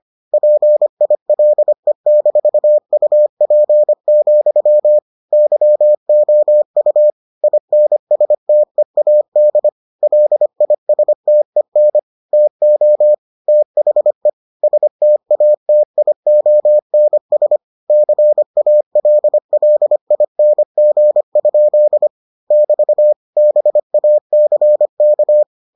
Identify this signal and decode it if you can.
Morse code, heard as a keyed tone, about 25 wpm, 600 Hz. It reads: PILE-UP, YOU INSTEAD LISTEN TO THE STATIONS CALLING? <BT> BACK